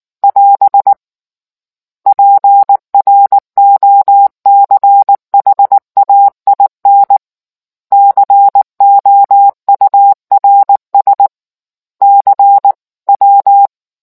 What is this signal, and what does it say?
Morse code, heard as a keyed tone, about 19 wpm, 800 Hz. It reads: <AS> PROCHAIN COURS CW